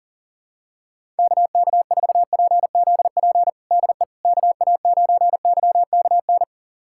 Morse code, heard as a keyed tone, about 40 wpm, 700 Hz. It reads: KK4PZP DE KA9YKN
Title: USA Callsign DE Callsign - 6 Characters - 40wpm